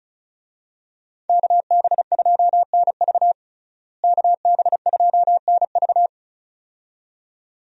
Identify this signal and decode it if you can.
Morse code, heard as a keyed tone, about 36 wpm, 700 Hz. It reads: KB2NV KB2NV
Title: USA Call Signs - 5 Characters - Repeated 2x - 35wpm